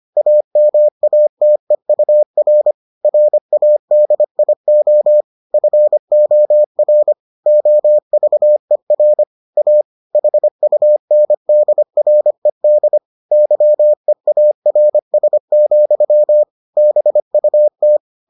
Morse code, heard as a keyed tone, about 25 wpm, 600 Hz. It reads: AMATEUR RADIO FOR OVER A HUNDRED YEARS, BUT